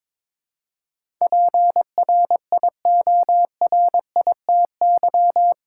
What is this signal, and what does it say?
Morse code, heard as a keyed tone, about 22 wpm, 700 Hz. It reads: PRIORITY
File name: practice / 22wpm / Eight Letter Words - 22wpm